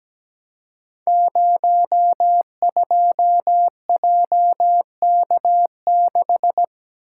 Morse code, heard as a keyed tone, about 17 wpm, 700 Hz. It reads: 02JK6
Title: Code Groups - 5 Letter:Numbers - 17wpm